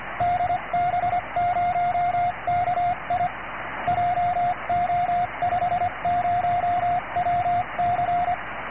Morse code, heard as a keyed tone, about 25 wpm, 710 Hz, with radio noise. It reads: DB0KI JO50WC
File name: ok1teh / db0ki 1442